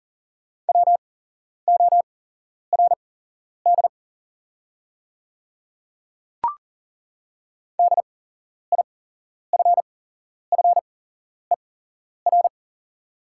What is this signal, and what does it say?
Morse code, heard as a keyed tone, about 40 wpm, spaced as for 11 wpm, 690 Hz. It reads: WORD DIFFER
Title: 8x Character Spacing - Top 200 Words - 40wpm